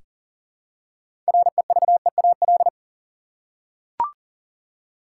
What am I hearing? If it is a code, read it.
Morse code, heard as a keyed tone, about 40 wpm, 700 Hz. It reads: REVEAL
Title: Six Letter Words - 40wpm